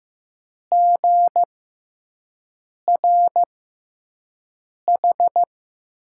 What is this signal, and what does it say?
Morse code, heard as a keyed tone, about 15 wpm, spaced as for 5 wpm, 700 Hz. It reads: GRH
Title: USA Call Signs - 5 Characters - 6x Character Spacing - 15wpm